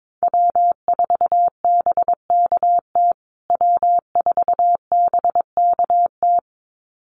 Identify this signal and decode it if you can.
Morse code, heard as a keyed tone, about 22 wpm, 700 Hz. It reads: W4BKT W4BKT
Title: USA Call Signs Repeated 2x 22wpm